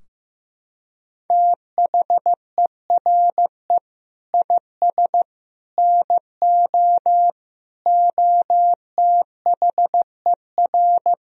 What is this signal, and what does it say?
Morse code, heard as a keyed tone, about 15 wpm, 700 Hz. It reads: THERE IS NO OTHER
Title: Sentences from Top 1000 Words - Review Part 4 - 15wpm